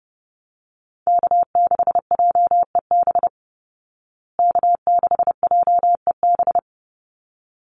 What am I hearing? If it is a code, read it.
Morse code, heard as a keyed tone, about 30 wpm, 700 Hz. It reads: K6JEB K6JEB
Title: Call Signs Repeated 3x - 30wpm